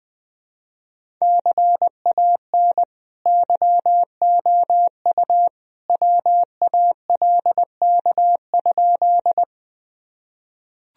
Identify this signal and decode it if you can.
Morse code, heard as a keyed tone, about 20 wpm, 700 Hz. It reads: CAN YOU WALK?
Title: Sentences from Top 1000 Words - Review Part 1 - 20wpm